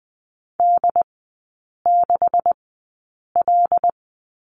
Morse code, heard as a keyed tone, about 20 wpm, spaced as for 8 wpm, 690 Hz. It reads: D6L
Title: 133 Lesson 22e - Question Mark - Three Characters 20wpm